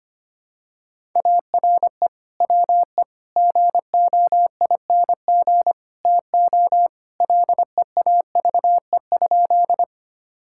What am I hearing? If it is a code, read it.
Morse code, heard as a keyed tone, about 25 wpm, 700 Hz. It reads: ARE WE GOING TO LEAVE?
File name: practice / 25wpm / Sentences with 4-Word Phrases - 25wpm